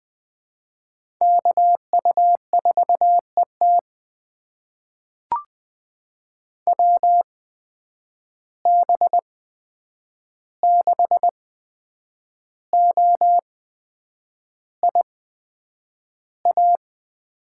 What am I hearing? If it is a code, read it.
Morse code, heard as a keyed tone, about 20 wpm, spaced as for 8 wpm, 700 Hz. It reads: KU4ET WB6OIA
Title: USA Call Signs - 8x Character Spacing - 20wpm